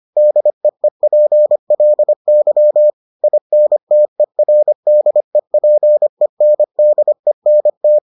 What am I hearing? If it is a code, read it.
Morse code, heard as a keyed tone, about 25 wpm, 600 Hz. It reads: DEEPLY INTERDEPENDENT